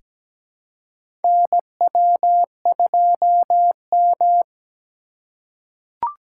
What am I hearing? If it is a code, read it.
Morse code, heard as a keyed tone, about 17 wpm, 700 Hz. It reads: NW2M
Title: USA Call Signs - 4x Character Spacing - 17wpm